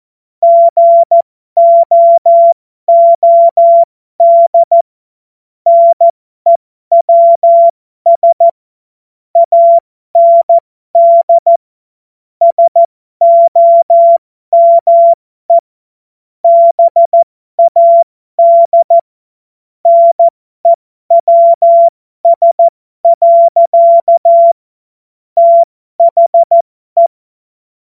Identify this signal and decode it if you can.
Morse code, heard as a keyed tone, about 14 wpm, spaced as for 12 wpm, 680 Hz. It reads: GOOD NEWS AND SOME BAD NEWS. THE
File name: Morsekurs / Witze